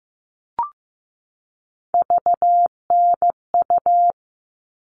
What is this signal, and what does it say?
Morse code, heard as a keyed tone, about 15 wpm, 700 Hz. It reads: VNU
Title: All Cut Numbers - 3-Digit 15wpm